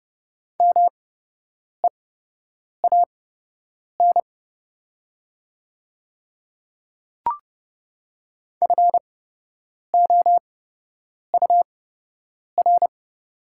Morse code, heard as a keyed tone, about 30 wpm, spaced as for 8 wpm, 700 Hz. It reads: MEAN FOUR
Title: 8x Character Spacing - Top 300 Words - 30wpm